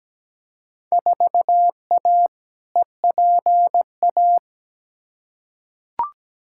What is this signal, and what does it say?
Morse code, heard as a keyed tone, about 17 wpm, 700 Hz. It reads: 4A EPA
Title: Field Day Exchanges - 17wpm